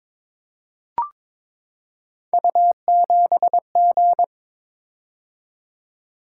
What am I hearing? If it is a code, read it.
Morse code, heard as a keyed tone, about 23 wpm, 700 Hz. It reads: U7G
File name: practice / 22wpm / Three Letter-Number Combinations - 22wpm